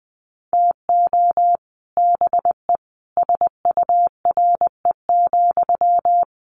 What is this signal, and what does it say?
Morse code, heard as a keyed tone, about 20 wpm, 700 Hz. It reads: TO BE SURE,
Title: Sentences from Top 1000 Words 20wpm